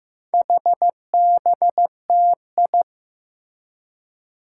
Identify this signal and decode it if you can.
Morse code, heard as a keyed tone, about 15 wpm, 700 Hz. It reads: HBTI
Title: Four Challenging Letter-Number Combinations 15wpm